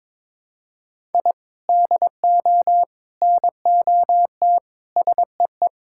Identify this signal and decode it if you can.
Morse code, heard as a keyed tone, about 22 wpm, 700 Hz. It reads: I DO NOT SEE
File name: practice / 22wpm / Build A Phrase - 5-Word Phrases - 22wpm